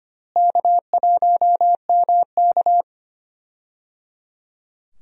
Morse code, heard as a keyed tone, about 25 wpm, 700 Hz. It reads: K1MK
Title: USA Call Signs - 4 Characters - Repeated 3x - 25wpm